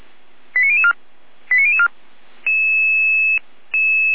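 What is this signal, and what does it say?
Five-tone selective call (ZVEI), heard as quick runs of five tones, tones of about 70 ms. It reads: 89004 89004
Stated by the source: ZVEI five-tone selective call received by SDR, logged by feuerwehr-orlamuende.org, a fire brigade.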